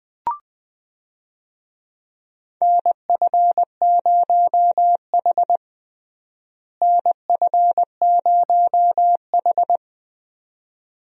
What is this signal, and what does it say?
Morse code, heard as a keyed tone, about 20 wpm, 700 Hz. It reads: NF0H NF0H
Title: USA Call Signs - 4 Characters - Repeated 2x - 20wpm